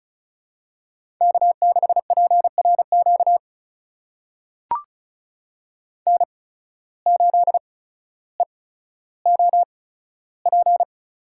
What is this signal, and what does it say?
Morse code, heard as a keyed tone, about 36 wpm, spaced as for 15 wpm, 700 Hz. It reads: K6PRQ N8EOP
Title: USA Call Signs - 5 Characters - 8x Character Spacing - 35wpm